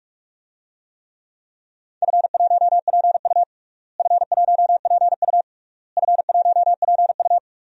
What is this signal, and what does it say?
Morse code, heard as a keyed tone, about 42 wpm, 700 Hz. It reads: F1PU F1PU F1PU
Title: DXCC-100 Call Signs - 4 Characters - Repeated 3x - 45wpm